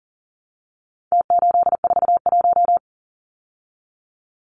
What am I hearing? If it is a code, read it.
Morse code, heard as a keyed tone, about 40 wpm, 700 Hz. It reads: T841
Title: Numbers - Contest Style - 4-Digit 40wpm